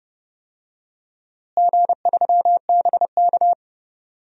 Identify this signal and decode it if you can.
Morse code, heard as a keyed tone, about 30 wpm, 700 Hz. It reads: G3BK
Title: DXCC-100 Call Signs - 4 Characters - Repeated 3x - 30wpm